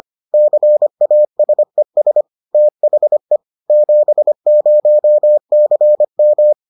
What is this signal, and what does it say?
Morse code, heard as a keyed tone, about 25 wpm, 600 Hz. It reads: CASES THE 70CM